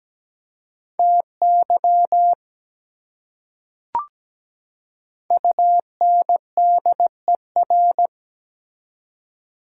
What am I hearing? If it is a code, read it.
Morse code, heard as a keyed tone, about 17 wpm, 700 Hz. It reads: TY UNDER
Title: Prefixes and Suffixes 17wpm